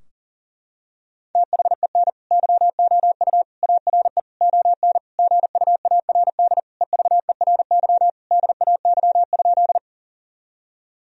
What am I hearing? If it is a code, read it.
Morse code, heard as a keyed tone, about 37 wpm, 700 Hz. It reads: THEN YOU ARE ON GUARD EVERY DAY?
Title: Sentences from Top 1000 Words - Review Part 4 - 40wpm